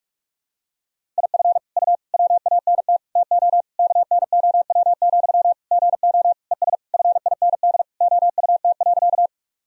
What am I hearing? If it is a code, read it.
Morse code, heard as a keyed tone, about 46 wpm, 700 Hz. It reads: IF U WANT TO KNOW, GO ES FIND OUT.
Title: Sentences from Top 200 Words and Abbreviations 45wpm